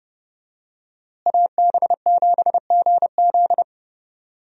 Morse code, heard as a keyed tone, about 32 wpm, 700 Hz. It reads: AB7GZ
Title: USA Call Signs - 5 Characters - Repeated 2x - 30wpm